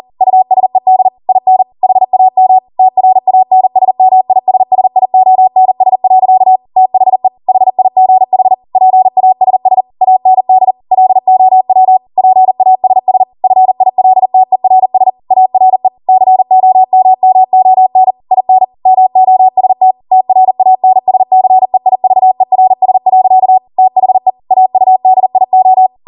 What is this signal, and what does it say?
Morse code, heard as a keyed tone, about 40 wpm, 750 Hz. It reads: USED IN HAM TRANSMISSIONS. THE HIGH PASS AND LOW PASS FILTERS ARE COMMON IN MOST TRANSCEIVERS. THE AUDIO